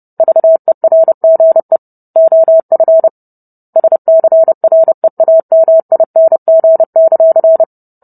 Morse code, heard as a keyed tone, about 30 wpm, 650 Hz. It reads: VERGE OF SCREAMING;